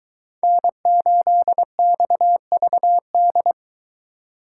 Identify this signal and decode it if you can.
Morse code, heard as a keyed tone, about 23 wpm, 700 Hz. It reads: N8XVD